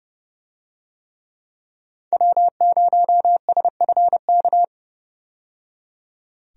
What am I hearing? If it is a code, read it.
Morse code, heard as a keyed tone, about 30 wpm, 700 Hz. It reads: W0SFK
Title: USA Call Signs - 5 Characters - 30wpm